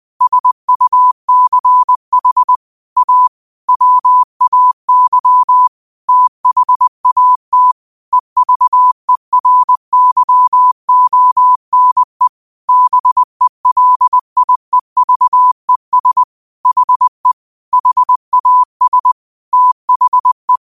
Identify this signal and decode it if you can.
Morse code, heard as a keyed tone, about 20 wpm, 1000 Hz. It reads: SUCH A WAY THAT EVERYONE BELIEVES HE HAS THE